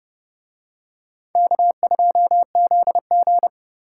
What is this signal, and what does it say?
Morse code, heard as a keyed tone, about 29 wpm, 700 Hz. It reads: K2ZG